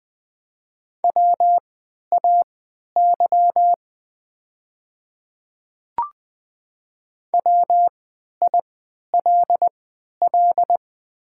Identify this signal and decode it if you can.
Morse code, heard as a keyed tone, about 20 wpm, spaced as for 11 wpm, 700 Hz. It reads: WAY WILL
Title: 3x Character Spacing - Top 100 Words - 20wpm